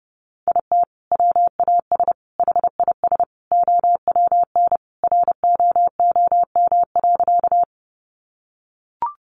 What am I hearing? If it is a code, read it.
Morse code, heard as a keyed tone, about 30 wpm, 700 Hz. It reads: IT WAS HIS OWN ROOM.
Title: Sentences from Top 300 Words 30wpm